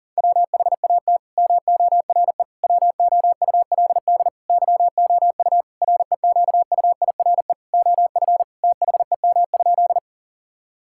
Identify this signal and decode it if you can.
Morse code, heard as a keyed tone, about 40 wpm, 700 Hz. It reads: WHAT MORE WOULD YOU REQUIRE OF THEM?
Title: Sentences from Top 1000 Words - Review Part 6 - 40wpm